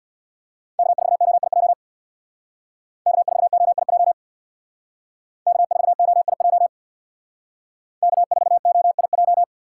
Morse code, heard as a keyed tone, about 64 wpm, 700 Hz. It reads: K4OIJ K4OIJ K4OIJ K4OIJ